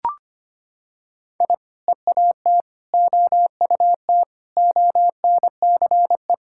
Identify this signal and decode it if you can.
Morse code, heard as a keyed tone, about 25 wpm, 700 Hz. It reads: I EAT OUT ONCE
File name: practice / 25wpm / Sentences with 4-Word Phrases - 25wpm